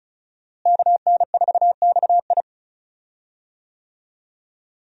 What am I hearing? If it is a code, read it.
Morse code, heard as a keyed tone, about 36 wpm, 700 Hz. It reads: KN4XI
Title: USA Call Signs - 5 Characters - Repeated 3x - 35wpm